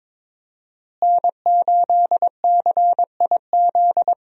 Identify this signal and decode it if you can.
Morse code, heard as a keyed tone, about 21 wpm, 700 Hz. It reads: N8CIZ
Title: USA Call Signs - 5 Characters - Repeated 2x - 22wpm